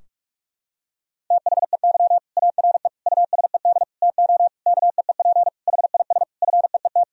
Morse code, heard as a keyed tone, about 46 wpm, 700 Hz. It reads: THEY ARE USED TO KEEP HIS FEET
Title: Sentences from Top 1000 Words - Review Part 3 - 45wpm